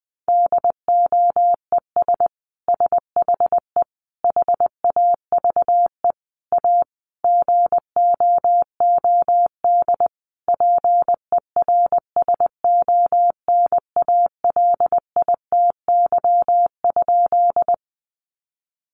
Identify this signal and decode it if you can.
Morse code, heard as a keyed tone, about 20 wpm, 700 Hz. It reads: DOES SHE HAVE A GOOD PERSONALITY?
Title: Sentences from Top 1400 Words 20wpm